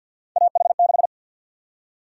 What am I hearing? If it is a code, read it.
Morse code, heard as a keyed tone, about 52 wpm, 700 Hz. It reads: AH6
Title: USA Call Sign Prefixes - 50wpm